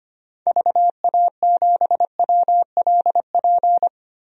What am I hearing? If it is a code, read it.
Morse code, heard as a keyed tone, about 25 wpm, 700 Hz. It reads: VA7WLP
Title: DXCC-100 Call Signs - 6 Characters - Repeated 2x - 25wpm